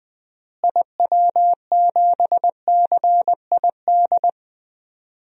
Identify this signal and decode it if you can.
Morse code, heard as a keyed tone, about 20 wpm, 700 Hz. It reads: IW7CID
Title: DXCC-100 Call Signs - 6 Characters - Repeated 2x - 20wpm